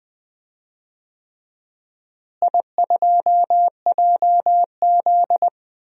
Morse code, heard as a keyed tone, about 20 wpm, 700 Hz. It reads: I2JZ
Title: DXCC-100 Call Signs - 4 Characters - Repeated 3x - 20wpm